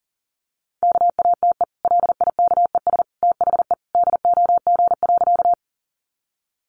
Morse code, heard as a keyed tone, about 40 wpm, 700 Hz. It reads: KATE LIKES THE DOG.
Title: Easy Phrases 40wpm